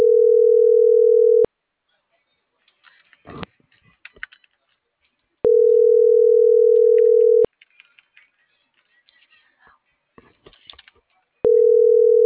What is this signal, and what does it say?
Ringback tone (North America), 440 + 480 Hz.
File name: CENTRAL NMP / out melanie.diaznmp - .36746